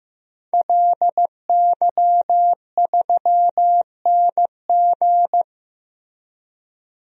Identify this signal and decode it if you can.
Morse code, heard as a keyed tone, about 15 wpm, 700 Hz. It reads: LY3NG
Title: DXCC-100 Call Signs - 5 Characters - Repeated 2x - 15wpm